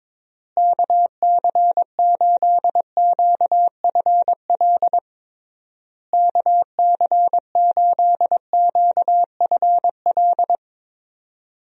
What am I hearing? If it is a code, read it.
Morse code, heard as a keyed tone, about 22 wpm, 700 Hz. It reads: KC8QFL KC8QFL
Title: USA Call Signs - 6 Characters - Repeated 3x - 22wpm